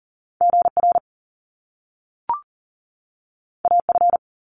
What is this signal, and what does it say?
Morse code, heard as a keyed tone, about 40 wpm, 700 Hz. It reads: GR AF
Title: Top Two Letter Combinations in Words 40wpm